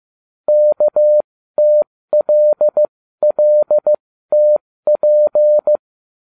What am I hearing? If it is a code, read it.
Morse code, heard as a keyed tone, about 15 wpm, spaced as for 13 wpm, 600 Hz. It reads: KTLLTP